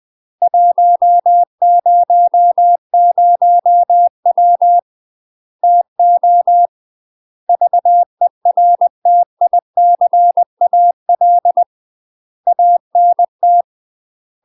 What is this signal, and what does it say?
Morse code, heard as a keyed tone, about 20 wpm, 700 Hz. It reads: 100W TO VERTICAL ANT
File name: UK MASTER QSOs TRAINING/20WPM